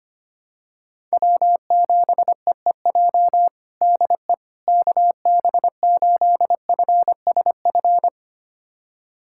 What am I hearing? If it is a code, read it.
Morse code, heard as a keyed tone, about 25 wpm, 700 Hz. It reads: W7EEJ DE KB8FSF
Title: USA Callsign DE Callsign 25wpm